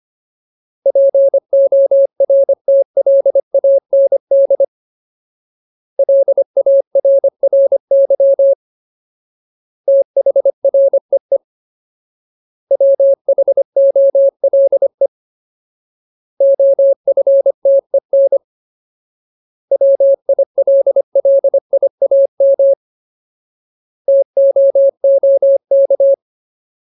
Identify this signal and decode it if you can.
Morse code, heard as a keyed tone, about 25 wpm, 550 Hz. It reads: PORTLAND LARRY THREE WHOLE OFTEN WILLIAM TOOK
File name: Ultimate Word List / 25WPM Ultimate